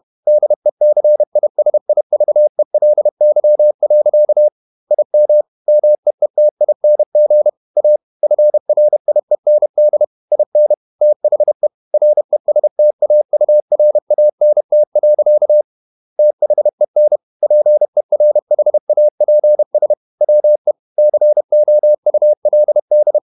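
Morse code, heard as a keyed tone, about 31 wpm, 600 Hz. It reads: DECISIVELY. IM MEETING A FRIEND IN THE RESTAURANT. THEN PERHAPS WE COULD
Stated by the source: Morse code at 30 wpm, the novel Never by Ken Follett in English, from CW med Gnister.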